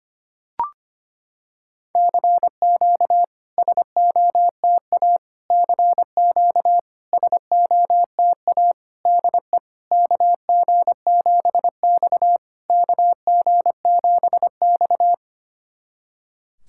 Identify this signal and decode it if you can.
Morse code, heard as a keyed tone, about 25 wpm, 690 Hz. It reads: CQ SOTA CQ SOTA DE KG7X KG7X
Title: SOTA QSO Practice 25wpm